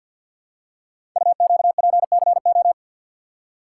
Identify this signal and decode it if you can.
Morse code, heard as a keyed tone, about 50 wpm, 700 Hz. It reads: UQPCO
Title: Code Groups - 5 Letters - 50wpm